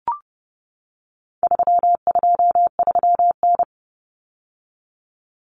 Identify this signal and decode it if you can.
Morse code, heard as a keyed tone, about 30 wpm, 700 Hz. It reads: 323N
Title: Common Cut Numbers - 4-Digit 30wpm